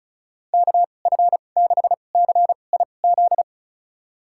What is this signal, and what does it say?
Morse code, heard as a keyed tone, about 36 wpm, spaced as for 25 wpm, 700 Hz. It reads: KF6CIZ
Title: USA Call Signs - 6 Characters - 2x Character Spacing - 35wpm